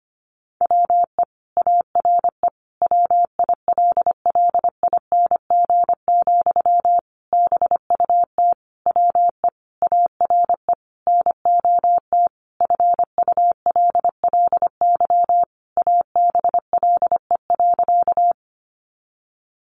Morse code, heard as a keyed tone, about 25 wpm, 700 Hz. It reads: WE ARE WILLING, BUT WE ARE NOT FULLY ABLE.
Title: Sentences from Top 1300 Words 25wpm